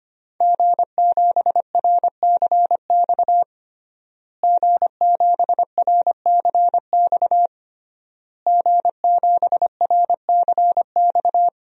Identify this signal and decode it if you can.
Morse code, heard as a keyed tone, about 25 wpm, 700 Hz. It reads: G7RCX G7RCX G7RCX
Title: DXCC-100 Call Signs - 5 Characters - Repeated 3x - 25wpm